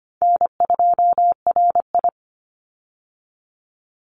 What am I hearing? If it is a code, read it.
Morse code, heard as a keyed tone, about 25 wpm, 700 Hz. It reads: N2RI
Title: Call Signs - 25wpm